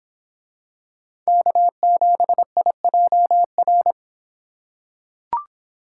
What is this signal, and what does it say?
Morse code, heard as a keyed tone, about 26 wpm, 700 Hz. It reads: K7IJR